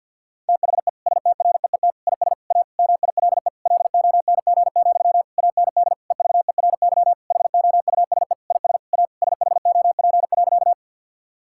Morse code, heard as a keyed tone, about 51 wpm, 700 Hz. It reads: THE STREET IS A MILE LONG, AND EVERY HOUSE IS A SHOP.
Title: Sentences from Top 1600 Words 50wpm